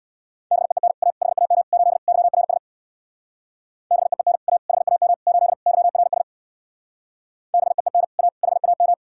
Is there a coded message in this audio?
Yes: Morse code from 0.5 to 8.9 s, heard as a keyed tone, about 64 wpm, 700 Hz. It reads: BEEN A HAM 8 YRS BEEN A HAM 8 YRS BEEN A HAM